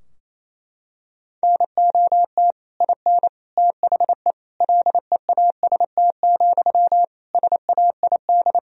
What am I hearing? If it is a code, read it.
Morse code, heard as a keyed tone, about 28 wpm, 700 Hz. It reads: NOT IN THE LEAST, SAID
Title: Sentences from Top 1000 Words - Review Part 4 - 28wpm